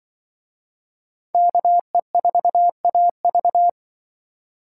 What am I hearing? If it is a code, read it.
Morse code, heard as a keyed tone, about 24 wpm, 700 Hz. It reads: KE4AV
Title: Speed Racing - USA 2x2 Call Signs 20wpm